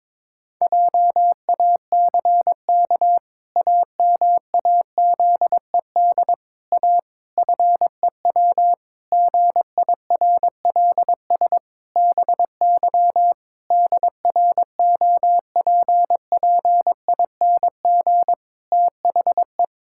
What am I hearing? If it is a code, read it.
Morse code, heard as a keyed tone, about 22 wpm, 700 Hz. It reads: JACK AMAZED A FEW GIRLS BY DROPPING THE
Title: Pangram Challenge v2 - 22wpm